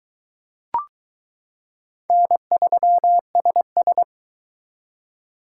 Morse code, heard as a keyed tone, about 21 wpm, 700 Hz. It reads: N3SS